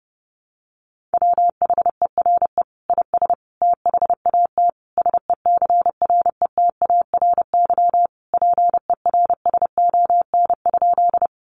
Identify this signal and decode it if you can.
Morse code, heard as a keyed tone, about 30 wpm, 700 Hz. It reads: WHERE IS THAT SECRETARY PERSON?
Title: Sentences from Top 1300 Words 30wpm